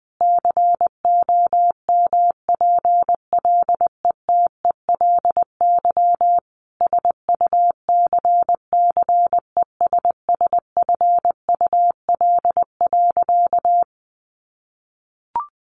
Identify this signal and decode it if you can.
Morse code, heard as a keyed tone, about 20 wpm, 700 Hz. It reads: COMPLETELY SUCCESSFUL.
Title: Sentences from Top 1100 Words 20wpm